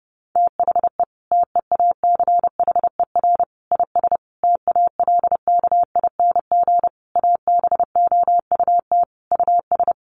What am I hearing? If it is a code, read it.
Morse code, heard as a keyed tone, about 30 wpm, 700 Hz. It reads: THE TEACHER IS TALKING ABOUT US
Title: Easy Phrases 30wpm